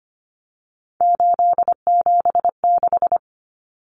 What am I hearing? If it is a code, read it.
Morse code, heard as a keyed tone, about 25 wpm, 700 Hz. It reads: 876
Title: Common Cut Numbers - 3-Digit 25wpm